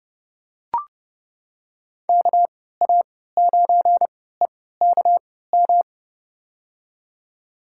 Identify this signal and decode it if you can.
Morse code, heard as a keyed tone, about 30 wpm, spaced as for 17 wpm, 700 Hz. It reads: KA9EKM
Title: USA Call Signs - 3x Character Spacing - 30wpm